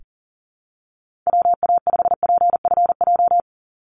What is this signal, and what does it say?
Morse code, heard as a keyed tone, about 40 wpm, 700 Hz. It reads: WA5PFJ
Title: Call Signs Repeated 3x - 40wpm